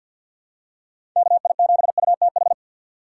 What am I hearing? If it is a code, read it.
Morse code, heard as a keyed tone, about 49 wpm, 700 Hz. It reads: KI7UTH
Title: USA Call Signs - 6 Characters - 4x Character Spacing - 50wpm